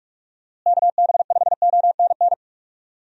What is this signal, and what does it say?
Morse code, heard as a keyed tone, about 42 wpm, 700 Hz. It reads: KB5ONN
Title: USA Call Signs - 6 Characters - Repeated 3x - 45wpm